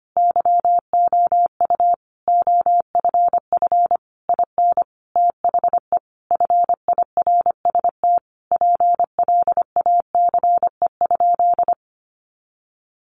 Morse code, heard as a keyed tone, about 25 wpm, 700 Hz. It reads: YOU OFF IN THE FIRST PLACE?
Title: Sentences from Top 200 Words 25wpm